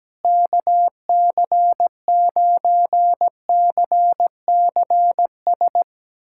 Morse code, heard as a keyed tone, about 17 wpm, 700 Hz. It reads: KC9CCS
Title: USA Callsign DE Callsign 17wpm